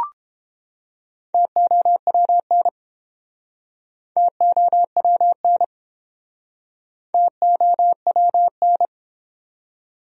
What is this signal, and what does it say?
Morse code, heard as a keyed tone, about 30 wpm, 700 Hz. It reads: TOWN TOWN TOWN